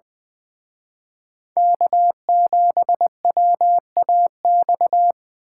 Morse code, heard as a keyed tone, about 24 wpm, 700 Hz. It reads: K7WAX
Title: Speed Racing - USA 1x3 Call Signs 20wpm